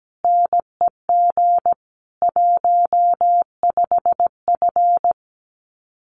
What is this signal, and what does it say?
Morse code, heard as a keyed tone, about 17 wpm, 700 Hz. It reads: NEG 15F
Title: Common QSO Phrases Encore 17wpm